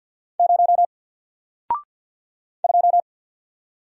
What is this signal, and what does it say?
Morse code, heard as a keyed tone, about 50 wpm, 700 Hz. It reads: 0 2
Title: Single Letter-Number 50wpm